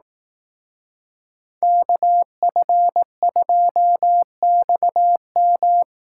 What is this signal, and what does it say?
Morse code, heard as a keyed tone, about 18 wpm, 700 Hz. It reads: KF2XM
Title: Speed Racing - USA 2x2 Call Signs 15wpm